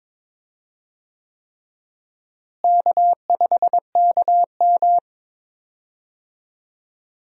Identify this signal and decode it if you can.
Morse code, heard as a keyed tone, about 22 wpm, 700 Hz. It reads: K5KM
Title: USA Call Signs - 4 Characters - Repeated 3x - 22wpm